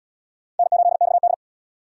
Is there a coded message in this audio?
Yes: Morse code from 0.6 to 1.3 s, heard as a keyed tone, about 78 wpm, 700 Hz. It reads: N8XD